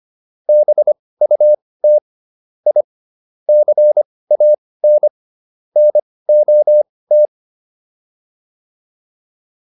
Morse code, heard as a keyed tone, about 25 wpm, spaced as for 18 wpm, 600 Hz. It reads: BUT I CAN NOT